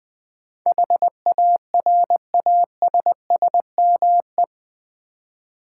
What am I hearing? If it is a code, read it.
Morse code, heard as a keyed tone, about 20 wpm, 700 Hz. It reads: HARASSME
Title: Build A Word - 4 Letters At A Time - 10-Letter Words - 20wpm